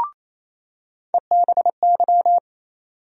Morse code, heard as a keyed tone, about 28 wpm, 700 Hz. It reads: EBY
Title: USA Call Sign Postfixes - 3 Letters 28wpm